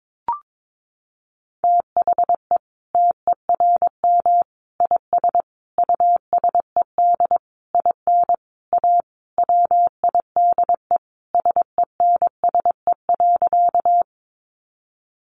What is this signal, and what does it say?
Morse code, heard as a keyed tone, about 22 wpm, 700 Hz. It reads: THE TERM IS USED IN A WIDE SENSE.
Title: Sentences from Top 900 Words 22wpm